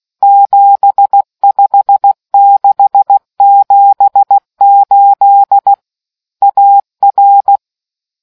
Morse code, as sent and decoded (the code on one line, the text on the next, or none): --... ..... -.... --... ---.. / .- .-.
75678 AR